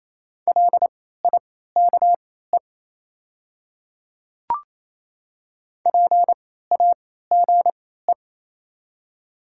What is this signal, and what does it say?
Morse code, heard as a keyed tone, about 28 wpm, spaced as for 16 wpm, 700 Hz. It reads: LIKE PAGE
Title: 3x Character Spacing - Top 300 Words - 28wpm